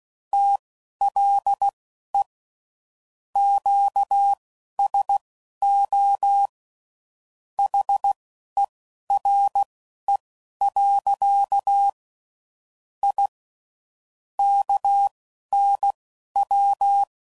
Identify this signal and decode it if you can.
Morse code, heard as a keyed tone, about 16 wpm, spaced as for 11 wpm, 800 Hz. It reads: TLE QSO HERE. I KNW
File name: CW Code Groups @10WPM / QSOs / QSO WD0FBT